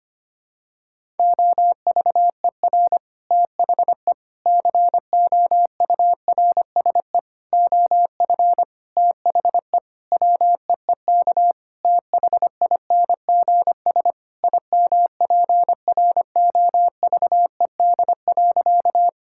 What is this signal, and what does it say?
Morse code, heard as a keyed tone, about 25 wpm, 700 Hz. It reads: OVER THE COURSE OF THE WEEK THINGS IMPROVED.
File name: practice / 25wpm / Sentences with 4-Word Phrases - 25wpm